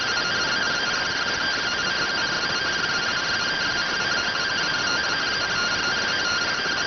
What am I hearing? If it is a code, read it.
Two-tone teleprinter signal, about 170 Hz shift.